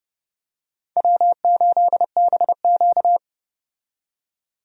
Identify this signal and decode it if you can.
Morse code, heard as a keyed tone, about 30 wpm, 700 Hz. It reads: W8BQ
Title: USA Call Signs - 4 Characters - 30wpm